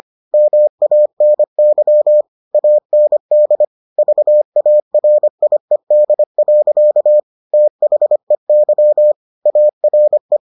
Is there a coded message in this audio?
Yes: Morse code from 0.3 to 10.4 s, heard as a keyed tone, about 25 wpm, 600 Hz. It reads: MANY AND VARIED. THEY ARE